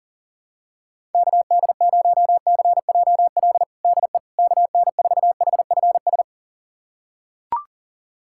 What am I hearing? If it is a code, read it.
Morse code, heard as a keyed tone, about 40 wpm, 700 Hz. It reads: KD0CJL DE KN4HFS
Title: USA Callsign DE Callsign 40wpm